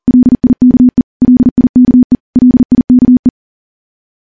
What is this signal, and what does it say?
Morse code, heard as a keyed tone, about 43 wpm, 250 Hz. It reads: LIKE LIKE LIKE
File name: morse / morse-training-40WPM